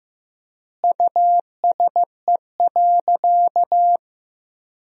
Morse code, heard as a keyed tone, about 15 wpm, 700 Hz. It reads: USE.
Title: 3x Word-Spacing - Sentences from Top 100 Words - 15wpm